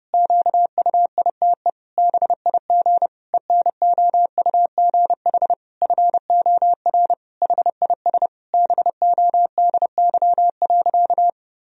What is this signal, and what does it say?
Morse code, heard as a keyed tone, about 30 wpm, 700 Hz. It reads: QUITE BIG ENOUGH FOR HIS BODY.
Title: Sentences from Top 1000 Words - Review Part 2 - 30wpm